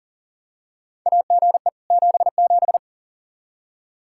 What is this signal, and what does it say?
Morse code, heard as a keyed tone, about 40 wpm, 700 Hz. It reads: AGE 77
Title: Practice QSO - Exchange 3 - More Info - 40wpm